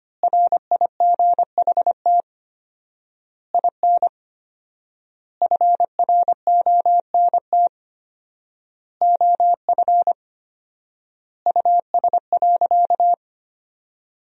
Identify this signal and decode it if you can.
Morse code, heard as a keyed tone, about 25 wpm, 700 Hz. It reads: RIGHT IN FRONT OF US.
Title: 4x Word-Spacing - Sentences from Top 500 Words - 25wpm